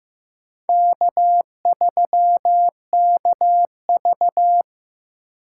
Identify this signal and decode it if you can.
Morse code, heard as a keyed tone, about 15 wpm, 700 Hz. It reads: K3KV
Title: USA Call Signs - 4 Characters - 2x Character Spacing - 15wpm